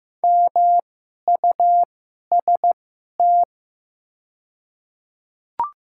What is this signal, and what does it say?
Morse code, heard as a keyed tone, about 15 wpm, spaced as for 11 wpm, 700 Hz. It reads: MUST
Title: 2x Character Spacing - Top 300 Words - 15wpm